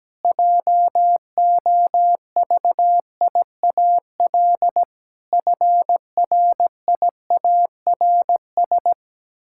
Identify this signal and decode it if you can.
Morse code, heard as a keyed tone, about 17 wpm, 700 Hz. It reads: JOVIAL FRIARS
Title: Pangram Challenge v2 - 17wpm